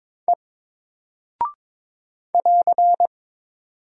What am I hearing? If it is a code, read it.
Morse code, heard as a keyed tone, about 22 wpm, 700 Hz. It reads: E <AR>
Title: Single Character (Letter:Number:Punctuation:Prosign) Alt 22wpm